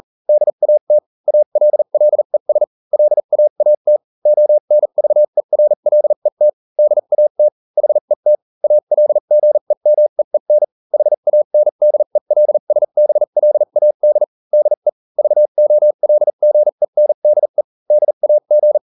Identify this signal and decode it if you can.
Morse code, heard as a keyed tone, about 40 wpm, spaced as for 35 wpm, 600 Hz. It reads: DAT ALLES LAAT ONVERLET DAT HET ALGEMEEN HANDELSBLAD DE VOLGENDE DAG